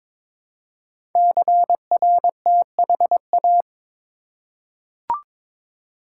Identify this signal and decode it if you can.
Morse code, heard as a keyed tone, about 23 wpm, 700 Hz. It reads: CRTHA
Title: Code Groups - 5 Letters - 22wpm